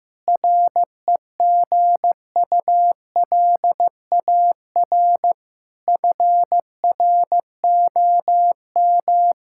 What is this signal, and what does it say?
Morse code, heard as a keyed tone, about 15 wpm, 700 Hz. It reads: REGULAR FROM
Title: Sentences from Top 1000 Words - Review Part 6 - 15wpm